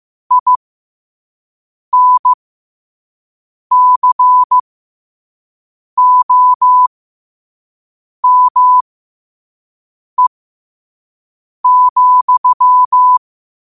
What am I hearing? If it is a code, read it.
Morse code, heard as a keyed tone, about 15 wpm, spaced as for 5 wpm, 1000 Hz. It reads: INCOME,